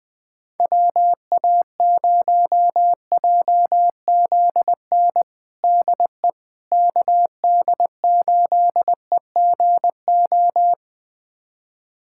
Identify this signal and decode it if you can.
Morse code, heard as a keyed tone, about 21 wpm, 700 Hz. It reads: WA0JZN DE KD8EGO
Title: USA Callsign DE Callsign 20wpm